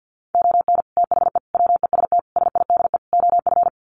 Morse code, heard as a keyed tone, about 48 wpm, 700 Hz. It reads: ON THE WEST SIDE OF
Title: Sets of 5 Words Encore2 50wpm